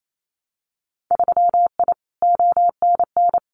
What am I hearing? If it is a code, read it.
Morse code, heard as a keyed tone, about 28 wpm, 700 Hz. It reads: 3I ONN
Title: Winter Field Day Exchanges Encore 28wpm